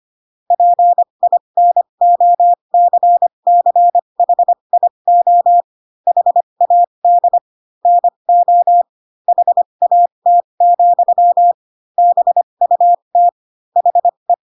Morse code, heard as a keyed tone, about 25 wpm, spaced as for 22 wpm, 700 Hz. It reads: PINOCCHIO HAD NO HAT, BUT HE
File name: Pinocchio / Pinocchio ch6 18 25 ik1hge